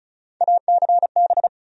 Morse code, heard as a keyed tone, about 34 wpm, 700 Hz. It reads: ACB
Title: USA Call Sign Postfixes - 3 Letters 35wpm